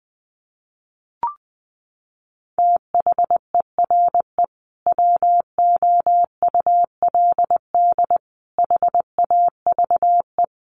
Morse code, heard as a keyed tone, about 20 wpm, 700 Hz. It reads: THERE WOULD HAVE
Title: Sentences from Top 1400 Words 20wpm